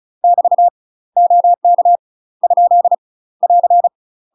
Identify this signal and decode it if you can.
Morse code, heard as a keyed tone, about 35 wpm, 700 Hz. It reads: <BT> OK ? <AR>
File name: UK MASTER QSOs TRAINING / 35WPM / UK MASTER QSOs 0043